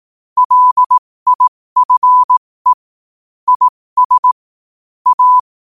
Morse code, heard as a keyed tone, about 18 wpm, spaced as for 15 wpm, 1000 Hz. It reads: LIFE IS A
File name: qotdfiles / 15wpm